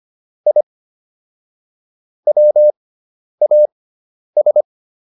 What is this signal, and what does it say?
Morse code, heard as a keyed tone, about 25 wpm, spaced as for 10 wpm, 600 Hz. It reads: I WAS